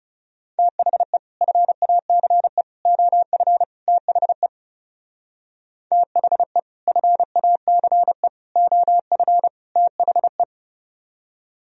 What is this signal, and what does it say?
Morse code, heard as a keyed tone, about 36 wpm, 700 Hz. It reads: THE FACE OF THE THE FACE OF THE
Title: Speed-Racing - Sets of 4 Words 30wpm